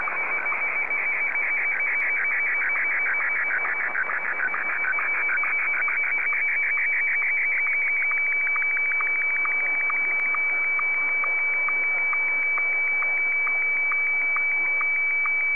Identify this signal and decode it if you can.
SSTV (slow-scan television), Martin 1, with speech.
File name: sstv / M1-VK3